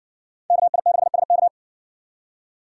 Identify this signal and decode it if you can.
Morse code, heard as a keyed tone, about 60 wpm, 700 Hz. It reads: KI6SK